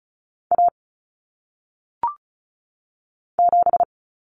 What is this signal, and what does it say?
Morse code, heard as a keyed tone, about 36 wpm, 700 Hz. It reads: A 7
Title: Common Cut Numbers - 1-Digit 35wpm